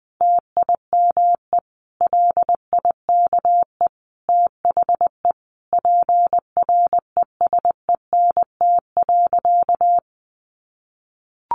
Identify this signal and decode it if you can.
Morse code, heard as a keyed tone, about 20 wpm, 700 Hz. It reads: TIME LIKE THE PRESENT.
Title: Sentences from Top 900 Words 20wpm